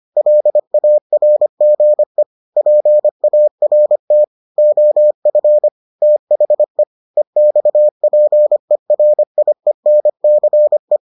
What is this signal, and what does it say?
Morse code, heard as a keyed tone, about 25 wpm, 600 Hz. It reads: LARGE PART OF THE EXPERIENCE